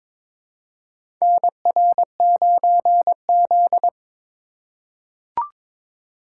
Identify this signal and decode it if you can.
Morse code, heard as a keyed tone, about 22 wpm, 700 Hz. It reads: NR9Z
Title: USA Call Signs - 4 Characters - 2x Character Spacing - 22wpm